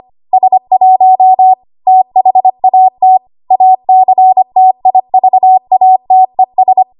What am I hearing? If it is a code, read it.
Morse code, heard as a keyed tone, about 25 wpm, 750 Hz. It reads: S1 THAT ACTIVATES
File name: Archive / 25 WPM / 25WPM